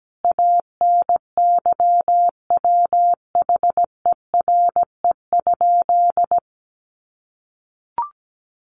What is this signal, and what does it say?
Morse code, heard as a keyed tone, about 17 wpm, 700 Hz. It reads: ANYWHERE?
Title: Sentences from Top 1400 Words 17wpm